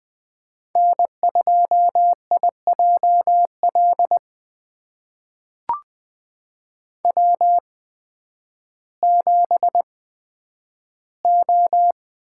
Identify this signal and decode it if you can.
Morse code, heard as a keyed tone, about 20 wpm, spaced as for 11 wpm, 700 Hz. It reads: N2IJL W7O
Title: USA Call Signs - 5 Characters - 8x Character Spacing - 20wpm